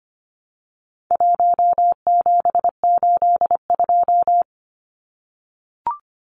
Morse code, heard as a keyed tone, about 25 wpm, 700 Hz. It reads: 1782
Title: Numbers - 4-Digit 25wpm